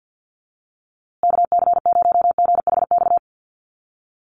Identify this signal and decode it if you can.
Morse code, heard as a keyed tone, about 50 wpm, 700 Hz. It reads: KC0GHX
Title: USA Call Signs Repeated 3x 50wpm